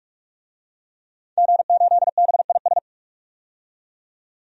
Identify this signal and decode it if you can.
Morse code, heard as a keyed tone, about 46 wpm, 700 Hz. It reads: G8BIS
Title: DXCC-100 Call Signs - 5 Characters - 45wpm